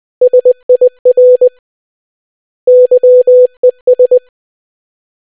Morse code, heard as a keyed tone, about 20 wpm, 500 Hz. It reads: SIR YES